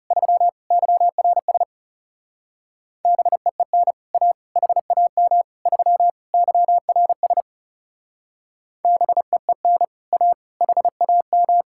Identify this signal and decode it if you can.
Morse code, heard as a keyed tone, about 34 wpm, 700 Hz. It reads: 3 YRS BEEN A HAM 3 YRS BEEN A HAM
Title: Speed-Racing Common QSO Phrases - 30wpm